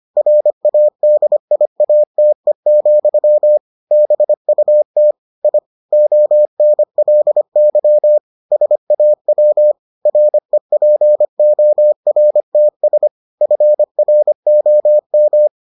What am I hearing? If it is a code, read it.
Morse code, heard as a keyed tone, about 25 wpm, 600 Hz. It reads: RADIATE, BUT I ONLY SAW REPORTS FROM